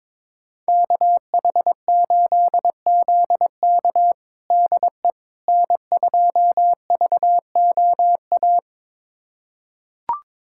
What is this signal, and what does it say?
Morse code, heard as a keyed tone, about 22 wpm, 700 Hz. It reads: KH8ZK DE N2VOA
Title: USA Callsign DE Callsign - 5 Characters - 22wpm